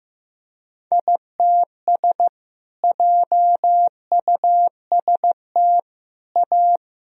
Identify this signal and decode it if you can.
Morse code, heard as a keyed tone, about 16 wpm, 700 Hz. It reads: ITS JUST A
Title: Sentences from Top 1000 Words - Review Part 4 - 15wpm